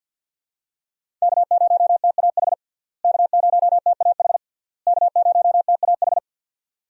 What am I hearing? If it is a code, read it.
Morse code, heard as a keyed tone, about 50 wpm, 700 Hz. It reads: K0TAH K0TAH K0TAH
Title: USA Call Signs - 5 Characters - Repeated 3x - 50wpm